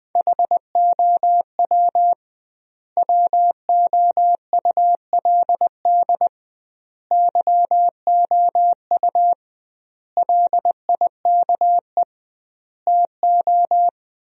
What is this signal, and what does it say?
Morse code, heard as a keyed tone, about 20 wpm, 700 Hz. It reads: HOW WOULD YOU LIKE TO
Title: 2x Word-Spacing - Sentences from Top 100 Words - 20wpm